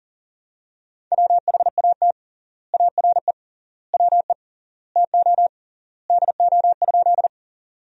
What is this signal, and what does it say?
Morse code, heard as a keyed tone, about 40 wpm, 700 Hz. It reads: WHAT ARE WE TO DO?
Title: 3x Word-Spacing - Sentences from Top 100 Words - 40wpm